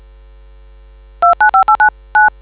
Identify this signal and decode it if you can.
DTMF, heard as telephone touch tones, tones of about 120 ms. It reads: 295899